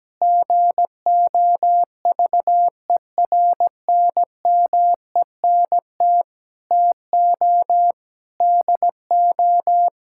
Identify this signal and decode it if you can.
Morse code, heard as a keyed tone, about 17 wpm, 700 Hz. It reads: GOVERNMENT TO DO
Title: Sentences from Top 1600 Words 17wpm